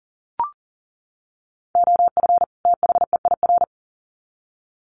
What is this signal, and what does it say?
Morse code, heard as a keyed tone, about 40 wpm, 700 Hz. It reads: OF THEIR
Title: Sets of 2 Words 40wpm